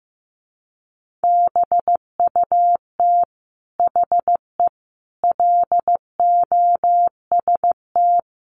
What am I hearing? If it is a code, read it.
Morse code, heard as a keyed tone, about 14 wpm, 700 Hz. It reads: BUT HE LOST
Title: Sentences from Top 1500 Words 15wpm